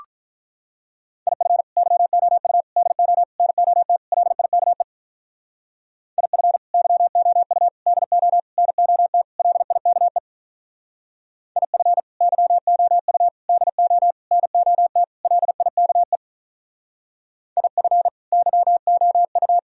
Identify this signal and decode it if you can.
Morse code, heard as a keyed tone, about 44 wpm, 700 Hz. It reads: IF YOU DO NOT LIKE IF YOU DO NOT LIKE IF YOU DO NOT LIKE IF YOU